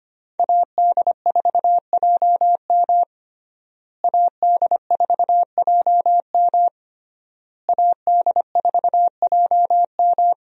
Morse code, heard as a keyed tone, about 25 wpm, 700 Hz. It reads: AD4JM AD4JM AD4JM
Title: USA Call Signs - 5 Characters - Repeated 3x - 25wpm